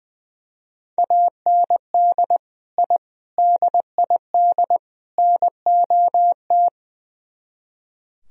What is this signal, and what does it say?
Morse code, heard as a keyed tone, about 20 wpm, 700 Hz. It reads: AND I DID NOT
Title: Speed-Racing - Sets of 4 Words 20wpm